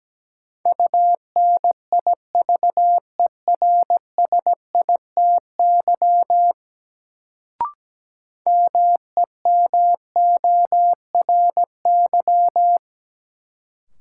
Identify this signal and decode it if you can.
Morse code, heard as a keyed tone, about 17 wpm, 700 Hz. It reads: UNIVERSITY MEMORY
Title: Sentences from Top 1300 Words - New Word Drill Alt - 17wpm